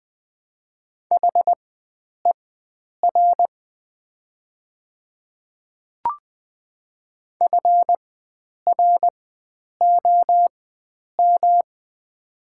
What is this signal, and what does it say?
Morse code, heard as a keyed tone, about 20 wpm, spaced as for 9 wpm, 700 Hz. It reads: HER FROM
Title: 4x Character Spacing - Top 100 Words - 20wpm